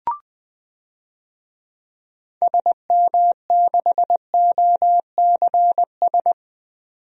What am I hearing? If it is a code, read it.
Morse code, heard as a keyed tone, about 20 wpm, 700 Hz. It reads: SM6OCS
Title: DXCC-100 Call Signs - 6 Characters - 20wpm